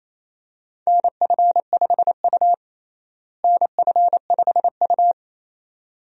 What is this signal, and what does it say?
Morse code, heard as a keyed tone, about 28 wpm, 700 Hz. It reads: NF5U NF5U
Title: USA Call Signs - 4 Characters - Repeated 2x - 28wpm